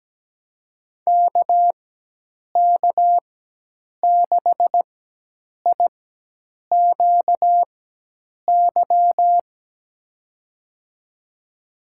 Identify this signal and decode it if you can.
Morse code, heard as a keyed tone, about 17 wpm, spaced as for 8 wpm, 700 Hz. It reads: KK6IQY
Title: USA Call Signs - 6 Characters - 4x Character Spacing - 17wpm